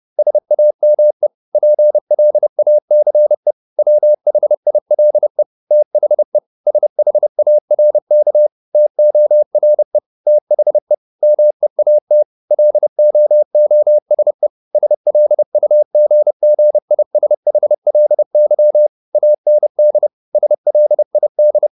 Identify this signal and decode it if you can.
Morse code, heard as a keyed tone, about 30 wpm, 600 Hz. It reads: SAME PLACE WHILE THE SHARK TORE THE MEAT LOOSE SLUGGISHLY AND SLID